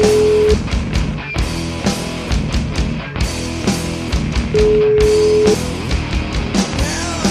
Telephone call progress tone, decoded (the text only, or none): ringback tone